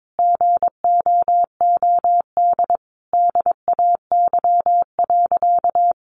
Morse code, heard as a keyed tone, about 22 wpm, 700 Hz. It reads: GOOD DAY.
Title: Sentences from Top 600 Words 22wpm